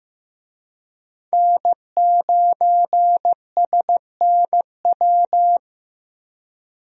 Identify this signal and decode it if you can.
Morse code, heard as a keyed tone, about 16 wpm, 700 Hz. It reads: N9SNW
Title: USA Call Signs - 5 Characters - 15wpm